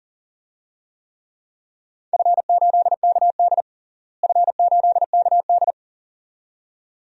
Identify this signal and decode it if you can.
Morse code, heard as a keyed tone, about 40 wpm, 700 Hz. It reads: F8KD F8KD
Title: DXCC-100 Call Signs - 4 Characters - Repeated 2x - 40wpm